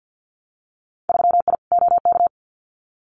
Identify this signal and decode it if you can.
Morse code, heard as a keyed tone, about 50 wpm, 700 Hz. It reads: 3I OK
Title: Winter Field Day Exchanges Encore 50wpm